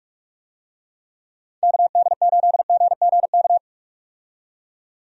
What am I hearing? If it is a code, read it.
Morse code, heard as a keyed tone, about 43 wpm, 700 Hz. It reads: KD8GGK
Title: USA Call Signs - 6 Characters - 45wpm